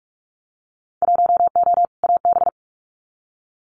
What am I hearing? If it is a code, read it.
Morse code, heard as a keyed tone, about 46 wpm, 700 Hz. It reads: 1O AB
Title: Winter Field Day Exchanges Encore 45wpm